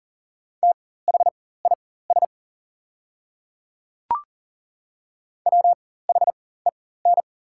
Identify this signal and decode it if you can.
Morse code, heard as a keyed tone, about 40 wpm, spaced as for 19 wpm, 700 Hz. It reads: THIS WHEN
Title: 4x Character Spacing - Top 100 Words - 40wpm